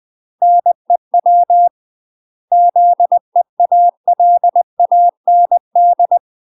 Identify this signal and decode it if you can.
Morse code, heard as a keyed tone, about 20 wpm, 700 Hz. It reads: NEW ZEALAND
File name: UK MASTER QSOs TRAINING / 20WPM / UK MASTER QSOs 0036